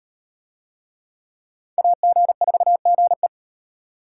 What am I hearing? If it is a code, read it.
Morse code, heard as a keyed tone, about 38 wpm, 700 Hz. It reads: AG4GE